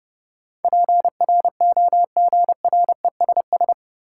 Morse code, heard as a keyed tone, about 30 wpm, 700 Hz. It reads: PROGRESS
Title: Sentences from Top 1400 Words - New Word Drill Alt - 30wpm